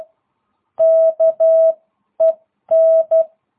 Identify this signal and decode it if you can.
Morse code, heard as a keyed tone, about 12 wpm, spaced as for 10 wpm, 660 Hz. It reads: KEN